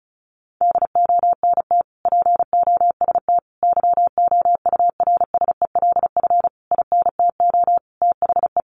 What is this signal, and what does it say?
Morse code, heard as a keyed tone, about 34 wpm, 700 Hz. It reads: DONT POST YOURSELF INTO THE
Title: Sentences from Top 1000 Words 35wpm